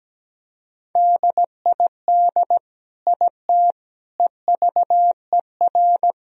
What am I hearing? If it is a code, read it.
Morse code, heard as a keyed tone, about 17 wpm, 700 Hz. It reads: DID IT EVER
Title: Sentences from Top 1000 Words - Review Part 2 - 17wpm